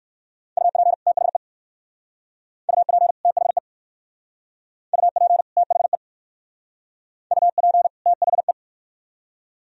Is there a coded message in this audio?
Yes: Morse code from 0.6 to 8.5 s, heard as a keyed tone, about 60 wpm, 690 Hz. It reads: UP THE UP THE UP THE UP THE